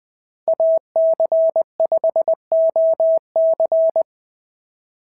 Morse code, heard as a keyed tone, about 19 wpm, 650 Hz. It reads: AC5OC
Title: POTA QSO Practice - 20wpm